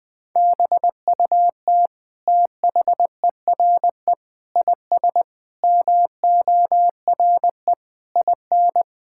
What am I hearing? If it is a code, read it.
Morse code, heard as a keyed tone, about 21 wpm, 700 Hz. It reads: BUT THERE IS MORE IN
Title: Sentences from Top 1000 Words - Review Part 5 - 20wpm